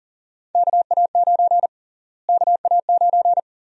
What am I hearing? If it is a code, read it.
Morse code, heard as a keyed tone, about 40 wpm, 700 Hz. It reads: KA9 KA9
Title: USA Call Sign Prefixes - Repeated 2x - 40wpm